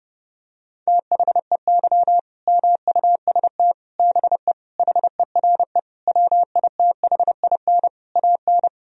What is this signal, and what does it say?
Morse code, heard as a keyed tone, about 30 wpm, 700 Hz. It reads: THEY MUST BE HERE WITHIN AN
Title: Sentences from Top 1000 Words - Review Part 4 - 30wpm